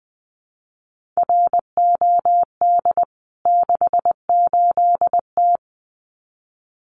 Morse code, heard as a keyed tone, about 20 wpm, 700 Hz. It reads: ROD 68T
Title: CWT Exchange 20wpm